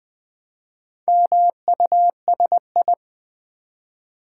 Morse code, heard as a keyed tone, about 20 wpm, 700 Hz. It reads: MUSI
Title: Build A Word - 1 Letter At A Time - 3 to 5-Letter Words - 20wpm